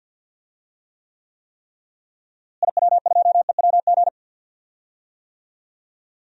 Morse code, heard as a keyed tone, about 53 wpm, 700 Hz. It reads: IW2EWG
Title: DXCC-100 Call Signs - 6 Characters - Repeated 3x - 50wpm